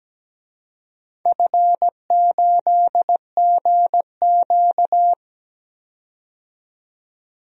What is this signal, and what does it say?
Morse code, heard as a keyed tone, about 17 wpm, 700 Hz. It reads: F8GQ
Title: DXCC-100 Call Signs - 4 Characters - 17wpm